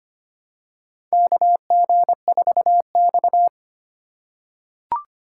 Morse code, heard as a keyed tone, about 25 wpm, 700 Hz. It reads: KG4X
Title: USA Call Signs - 4 Characters - 3x Character Spacing - 25wpm